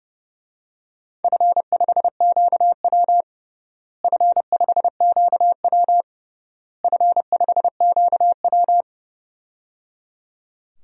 Morse code, heard as a keyed tone, about 30 wpm, 700 Hz. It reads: F5QW F5QW F5QW
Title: DXCC-100 Call Signs - 4 Characters - Repeated 3x - 30wpm